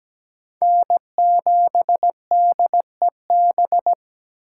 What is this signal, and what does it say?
Morse code, heard as a keyed tone, about 17 wpm, 700 Hz. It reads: N7DEB
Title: USA Call Signs - 5 Characters - 3x Character Spacing - 17wpm